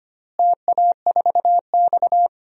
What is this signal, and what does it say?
Morse code, heard as a keyed tone, about 25 wpm, 700 Hz. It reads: TA4X
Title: DXCC-100 Call Signs - 4 Characters - Repeated 3x - 25wpm